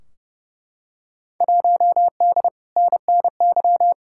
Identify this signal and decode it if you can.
Morse code, heard as a keyed tone, about 30 wpm, 700 Hz. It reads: 1D NNY
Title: Field Day Exchanges - 30wpm